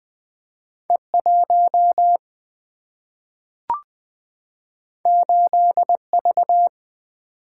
Morse code, heard as a keyed tone, about 20 wpm, 700 Hz. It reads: E1 8V
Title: Single-Letter Single-Number 20wpm